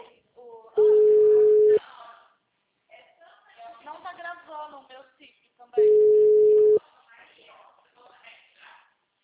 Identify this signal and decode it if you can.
Ringback tone (Europe, 425 Hz), with speech.